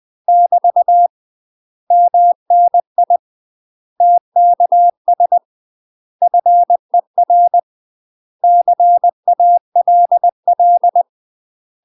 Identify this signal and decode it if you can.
Morse code, heard as a keyed tone, about 20 wpm, 700 Hz. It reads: <BT> MNI TKS FER CALL
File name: UK MASTER QSOs TRAINING / 20WPM / UK MASTER QSOs 0047